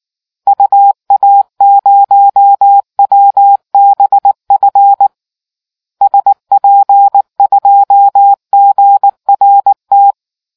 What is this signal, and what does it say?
Morse code, heard as a keyed tone, about 19 wpm, 800 Hz. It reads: UA0WBF SP2GRT